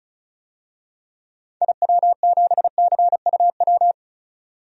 Morse code, heard as a keyed tone, about 34 wpm, 700 Hz. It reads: IW7CUW
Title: DXCC-100 Call Signs - 6 Characters - 35wpm